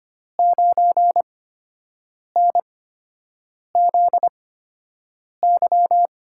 Morse code, heard as a keyed tone, about 25 wpm, spaced as for 7 wpm, 700 Hz. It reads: 9NZY
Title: USA Call Signs - 8x Character Spacing - 25wpm